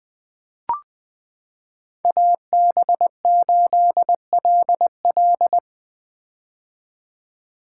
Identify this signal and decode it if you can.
Morse code, heard as a keyed tone, about 20 wpm, 700 Hz. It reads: AB8LL
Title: 2023 POTA Hunter Call Signs - 20wpm